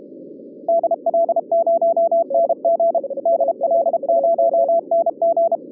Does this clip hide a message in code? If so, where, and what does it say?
Yes: Morse code from 0.7 to 5.6 s, heard as a keyed tone, about 32 wpm, 670 Hz, with radio noise. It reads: DL0NG DL0NG